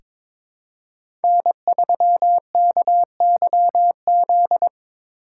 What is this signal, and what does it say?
Morse code, heard as a keyed tone, about 22 wpm, 700 Hz. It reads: N3KYZ
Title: USA Call Signs - 2x Character Spacing - 22wpm